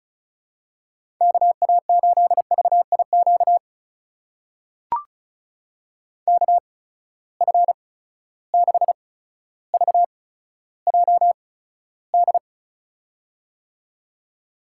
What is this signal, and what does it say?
Morse code, heard as a keyed tone, about 36 wpm, spaced as for 15 wpm, 700 Hz. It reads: KA8VIQ KF6VJD
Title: USA Call Signs - 6 Characters - 8x Character Spacing - 35wpm